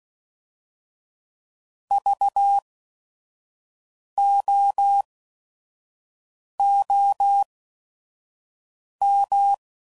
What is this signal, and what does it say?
Morse code, heard as a keyed tone, about 16 wpm, spaced as for 5 wpm, 800 Hz. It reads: VOOM
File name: Prefixes / By First Letter / Vs